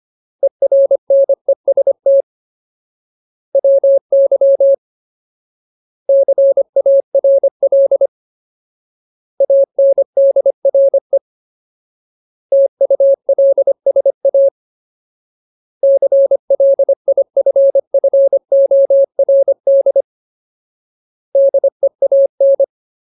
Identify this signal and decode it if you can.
Morse code, heard as a keyed tone, about 25 wpm, 550 Hz. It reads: ERNEST WY CARL ANDRE TULSA CLIFFORD DEAN
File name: Ultimate Word List / 25WPM Ultimate